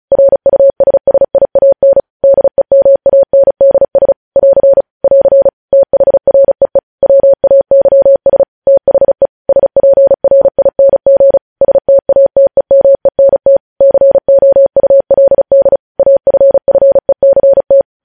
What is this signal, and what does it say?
Morse code, heard as a keyed tone, about 36 wpm, 550 Hz. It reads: RUSSIAN DEMANDS <AR> <AR> THREE WAYS THE SPRING STATEMENT COULD AFFECT
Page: News Headlines in Morse Code at 35 WPM